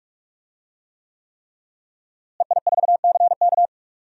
Morse code, heard as a keyed tone, about 43 wpm, 700 Hz. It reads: EI4CK
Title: DXCC-100 Call Signs - 5 Characters - Repeated 3x - 45wpm